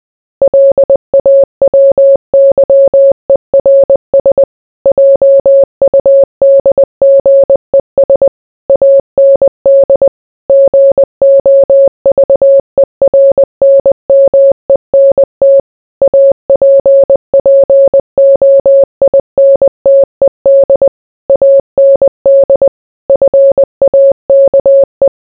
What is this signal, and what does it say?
Morse code, heard as a keyed tone, about 20 wpm, 550 Hz. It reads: LAWYERS JUDGES AND GOVERNMENT APPOINTED AND FAKE